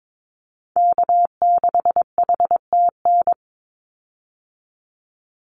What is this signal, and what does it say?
Morse code, heard as a keyed tone, about 22 wpm, 700 Hz. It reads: K6HTN
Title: Call Signs - 22wpm